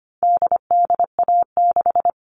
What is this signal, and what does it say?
Morse code, heard as a keyed tone, about 25 wpm, 700 Hz. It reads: DDA6
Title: All Cut Numbers - 4-Digit 25wpm